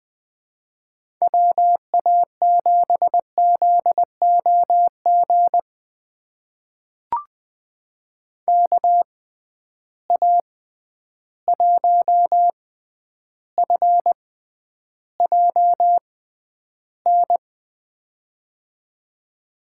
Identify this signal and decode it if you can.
Morse code, heard as a keyed tone, about 20 wpm, spaced as for 10 wpm, 700 Hz. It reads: WA7ZOG KA1FJN
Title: USA Call Signs - 6 Characters - 6x Character Spacing - 20wpm